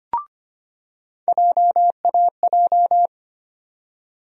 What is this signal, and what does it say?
Morse code, heard as a keyed tone, about 25 wpm, 700 Hz. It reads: JAJ
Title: USA Call Sign Postfixes - 3 Letters 25wpm